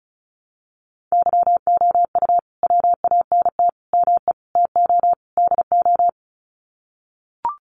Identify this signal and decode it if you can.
Morse code, heard as a keyed tone, about 35 wpm, 700 Hz. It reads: YOU WANT ME TO DO
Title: Sets of 5 Words Encore2 35wpm